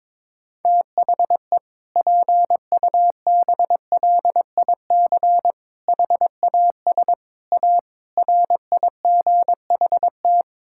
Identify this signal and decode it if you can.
Morse code, heard as a keyed tone, about 22 wpm, 700 Hz. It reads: THE PUBLIC HAS A RIGHT
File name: practice / 22wpm / Sets of 7 Words - 22wpm